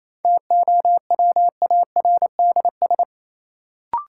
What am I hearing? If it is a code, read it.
Morse code, heard as a keyed tone, about 28 wpm, 700 Hz. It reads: TOWARDS
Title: Sentences from Top 1000 Words - Review Part 2 - New Word Drill - 28wpm